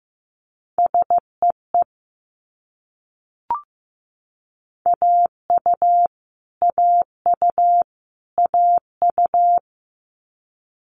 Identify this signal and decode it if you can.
Morse code, heard as a keyed tone, about 15 wpm, 700 Hz. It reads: SEE AU AU AU
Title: Multi-letter Phonemes Repeated 3x 15wpm